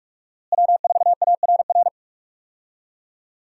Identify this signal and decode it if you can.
Morse code, heard as a keyed tone, about 46 wpm, 700 Hz. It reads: W4ARR
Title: USA Call Signs - 5 Characters - Repeated 3x - 45wpm